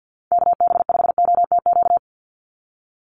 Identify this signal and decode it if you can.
Morse code, heard as a keyed tone, about 50 wpm, 700 Hz. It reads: KB5OTQ
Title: USA Call Signs 50wpm